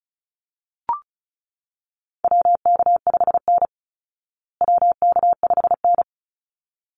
Morse code, heard as a keyed tone, about 34 wpm, 700 Hz. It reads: WK5N WK5N
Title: Call Signs Repeated 3x - 35wpm